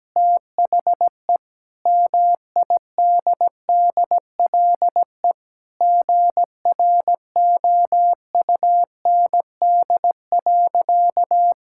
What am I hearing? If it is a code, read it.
Morse code, heard as a keyed tone, about 17 wpm, 700 Hz. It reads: THE MIDDLE GROUND.
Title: Sentences from Top 1600 Words 17wpm